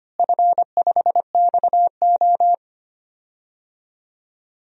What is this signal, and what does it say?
Morse code, heard as a keyed tone, about 25 wpm, 700 Hz. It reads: F5XO
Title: DXCC-100 Call Signs - 4 Characters - Repeated 2x - 25wpm